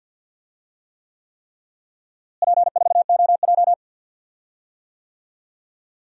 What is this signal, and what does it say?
Morse code, heard as a keyed tone, about 49 wpm, 700 Hz. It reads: P4OJ
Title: DXCC-100 Call Signs - 4 Characters - Repeated 3x - 50wpm